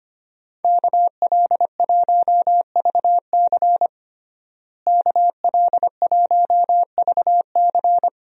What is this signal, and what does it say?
Morse code, heard as a keyed tone, about 25 wpm, 700 Hz. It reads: KL1VC KL1VC
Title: USA Call Signs - 5 Characters - Repeated 2x - 25wpm